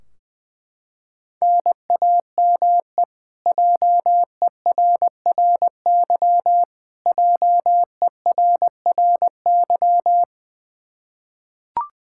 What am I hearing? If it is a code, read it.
Morse code, heard as a keyed tone, about 20 wpm, 700 Hz. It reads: NAME JERRY JERRY
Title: Practice QSO - Exchange 1 - Beginning - 20wpm